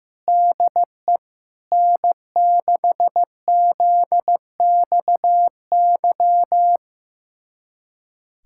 Morse code, heard as a keyed tone, about 16 wpm, 700 Hz. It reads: DE N6ZXY
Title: USA Callsign DE Callsign 15wpm